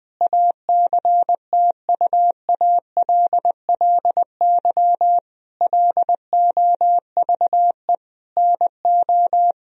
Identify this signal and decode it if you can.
Morse code, heard as a keyed tone, about 20 wpm, 700 Hz. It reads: ACTUALLY LOVE NO
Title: Sentences from Top 1000 Words - Review Part 4 - 20wpm